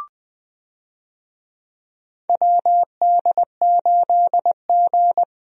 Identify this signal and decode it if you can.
Morse code, heard as a keyed tone, about 21 wpm, 700 Hz. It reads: WD8G
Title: USA Call Signs - 4 Characters - Repeated 2x - 20wpm